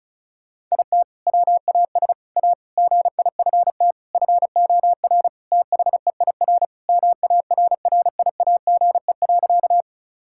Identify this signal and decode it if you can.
Morse code, heard as a keyed tone, about 34 wpm, 700 Hz. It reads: IT WAS A GIFT FOR THEIR MARRIAGE.
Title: Sentences from Top 1000 Words - Review Part 6 - 35wpm